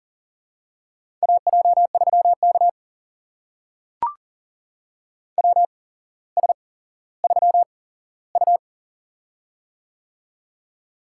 Morse code, heard as a keyed tone, about 42 wpm, spaced as for 17 wpm, 700 Hz. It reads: AJ3K WS3U
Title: USA Call Signs - 4 Characters - 8x Character Spacing - 40wpm